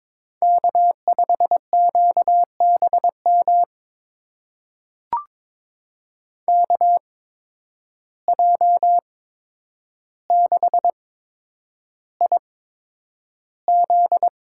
Morse code, heard as a keyed tone, about 22 wpm, spaced as for 9 wpm, 700 Hz. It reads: K5QBM KJ6IZ
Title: USA Call Signs - 5 Characters - 8x Character Spacing - 22wpm